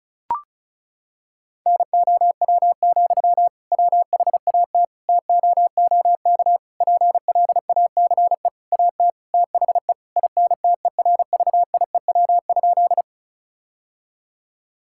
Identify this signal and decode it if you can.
Morse code, heard as a keyed tone, about 34 wpm, 700 Hz. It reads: NOW, WHAT TOOK PLACE AT THE INTERVIEW?
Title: Sentences from Top 1600 Words 35wpm